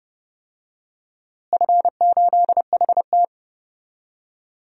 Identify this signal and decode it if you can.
Morse code, heard as a keyed tone, about 32 wpm, 700 Hz. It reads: F8HT
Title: DXCC-100 Call Signs - 4 Characters - Repeated 2x - 30wpm